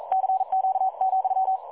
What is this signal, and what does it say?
Morse code, heard as a keyed tone, about 42 wpm, 760 Hz. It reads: CQ,
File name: dlazaawansowanych / nagraniazpasma / websdr recording 3565.0kHz